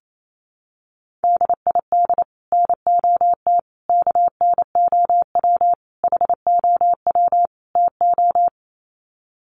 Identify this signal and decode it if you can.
Morse code, heard as a keyed tone, about 28 wpm, 700 Hz. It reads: DID NOT KNOW HOW TO
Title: Sets of 5 Words 28wpm